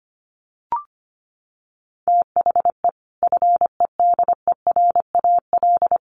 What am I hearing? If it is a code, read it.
Morse code, heard as a keyed tone, about 25 wpm, 700 Hz. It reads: THE FEDERAL
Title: Sets of 3 Words 25wpm